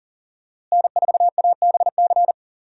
Morse code, heard as a keyed tone, about 37 wpm, 700 Hz. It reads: N4ABC
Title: USA Call Signs - 5 Characters - Repeated 3x - 40wpm